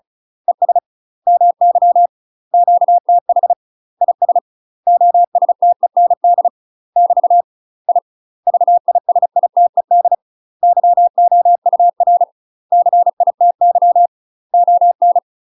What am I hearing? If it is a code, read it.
Morse code, heard as a keyed tone, about 34 wpm, 700 Hz. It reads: ES MY QTH IS OSTEND <BT> I VISITED YOUR CITY ON
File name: UK MASTER QSOs TRAINING / 35WPM / UK MASTER QSOs 0025